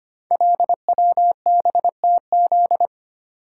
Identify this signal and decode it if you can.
Morse code, heard as a keyed tone, about 25 wpm, 700 Hz. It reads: LWBTZ
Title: Code Groups - 5 Letter:Numbers - 25wpm